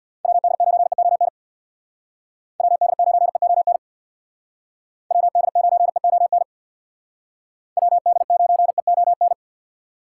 Morse code, heard as a keyed tone, about 62 wpm, 700 Hz. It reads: WD9EON WD9EON WD9EON WD9EON
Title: Speed Racing - USA Call Signs 50wpm